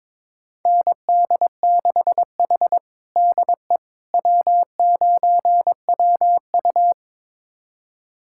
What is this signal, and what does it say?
Morse code, heard as a keyed tone, about 22 wpm, 700 Hz. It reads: ND6H DE W9WU
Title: USA Callsign DE Callsign - 4 Characters - 22wpm